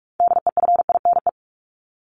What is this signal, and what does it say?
Morse code, heard as a keyed tone, about 44 wpm, 700 Hz. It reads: DEFINE
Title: ED Suffix - Top 100 Words with Base Words 45wpm